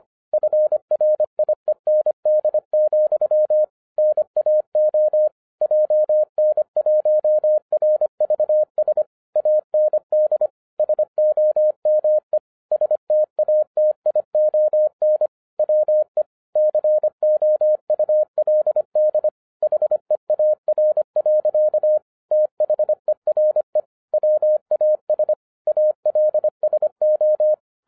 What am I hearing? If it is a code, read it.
Morse code, heard as a keyed tone, about 26 wpm, 610 Hz. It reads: FRIEND, NAO JN1RVS AND SOME STATION WE COULD HEAR. THERE WAS ALSO